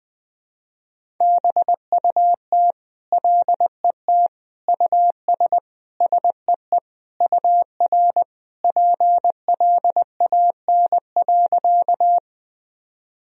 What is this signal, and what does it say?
Morse code, heard as a keyed tone, about 20 wpm, 700 Hz. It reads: BUT LET US SEE UR PLAN.
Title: Sentences from Top 500 Words and Abbreviations 20wpm